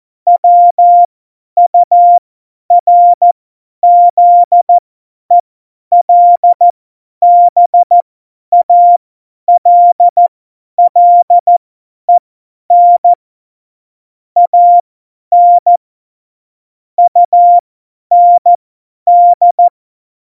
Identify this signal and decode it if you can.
Morse code, heard as a keyed tone, about 14 wpm, spaced as for 10 wpm, 700 Hz. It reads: WURZELBALLEN AN UND
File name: Morsekurs / Witze / Witz 17. November 2020, 50 F70